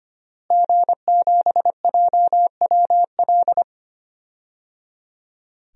Morse code, heard as a keyed tone, about 25 wpm, 700 Hz. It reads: G7JWL
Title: DXCC-100 Call Signs - 5 Characters - Repeated 3x - 25wpm